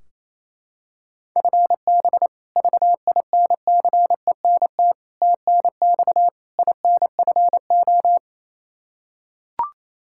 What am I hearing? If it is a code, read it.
Morse code, heard as a keyed tone, about 28 wpm, 700 Hz. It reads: FB VINCENT TNX INFO
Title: Practice QSO - Exchange 3 - More Info - 28wpm